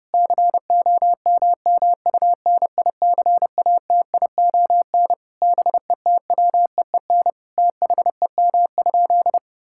Morse code, heard as a keyed tone, about 30 wpm, 700 Hz. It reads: COMMUNICATION BETWEEN THEM?
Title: Sentences from Top 1000 Words - Review Part 6 - 30wpm